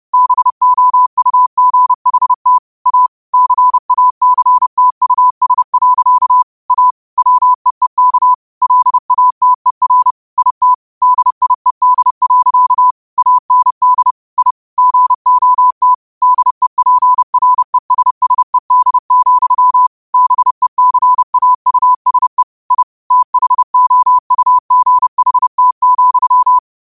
Morse code, heard as a keyed tone, about 30 wpm, 1000 Hz. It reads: BOUGHT A CACTUS. A WEEK LATER IT DIED. AND I GOT DEPRESSED, BECAUSE I THOUGHT,
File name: qotdfiles / 30wpm